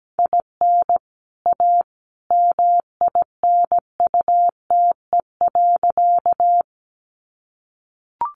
Morse code, heard as a keyed tone, about 17 wpm, 700 Hz. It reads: IN A MINUTE.
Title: Sentences from Top 500 Words 17wpm